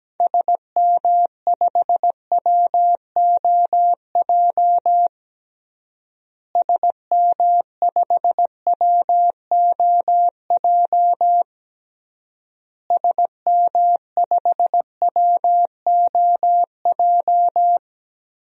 Morse code, heard as a keyed tone, about 17 wpm, 700 Hz. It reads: SM5WOJ SM5WOJ SM5WOJ
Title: DXCC-100 Call Signs - 6 Characters - Repeated 3x - 17wpm